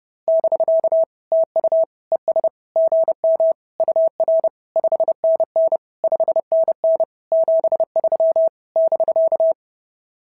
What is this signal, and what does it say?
Morse code, heard as a keyed tone, about 30 wpm, 690 Hz. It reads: <BK> TU ES GM UR 5NN 5NN 73 <BK>
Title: SOTA QSO Practice 30wpm